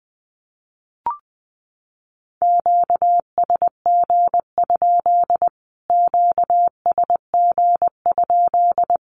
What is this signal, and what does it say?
Morse code, heard as a keyed tone, about 20 wpm, 700 Hz. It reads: QSG? QSG?
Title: Q-Codes Repeated 3x 20wpm